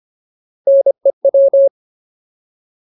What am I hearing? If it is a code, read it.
Morse code, heard as a keyed tone, about 25 wpm, 550 Hz. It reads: NEW
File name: Ultimate Word List / 25WPM Ultimate